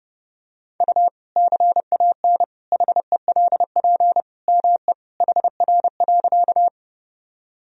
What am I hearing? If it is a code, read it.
Morse code, heard as a keyed tone, about 32 wpm, 700 Hz. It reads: U CAN HELP ME HR.
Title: Sentences from Top 200 Words and Abbreviations 30wpm